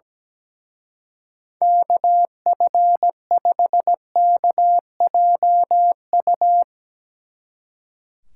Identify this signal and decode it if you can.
Morse code, heard as a keyed tone, about 17 wpm, 700 Hz. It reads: KF5KJU
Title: Speed Racing - USA 2x3 Call Signs 17wpm